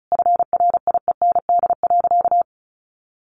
Morse code, heard as a keyed tone, about 34 wpm, 700 Hz. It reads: FRIEND.
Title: Sentences from Top 1300 Words 35wpm